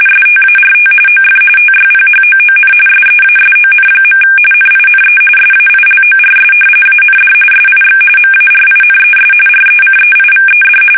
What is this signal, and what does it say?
RTTY, about 45.45 baud. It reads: START OF TYPHOON SEASON. NINE SAILORS AND SIX MARINES RECEIVED
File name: fsk850 / usnavy3